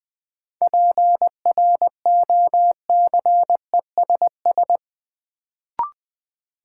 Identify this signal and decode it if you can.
Morse code, heard as a keyed tone, about 20 wpm, 700 Hz. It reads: PROCESS
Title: Sentences from Top 1000 Words - Review Part 3 - New Word Drill - 20wpm